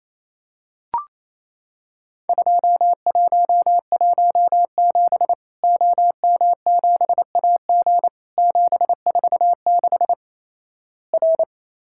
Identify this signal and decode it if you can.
Morse code, heard as a keyed tone, about 28 wpm, 700 Hz. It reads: 2117 OM7AG 746 R
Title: Worked All Europe DX Contest - QTC Exchange - 28wpm